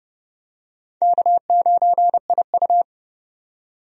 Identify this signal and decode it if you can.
Morse code, heard as a keyed tone, about 30 wpm, 700 Hz. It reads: K9IU
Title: USA Call Signs - 4 Characters - 30wpm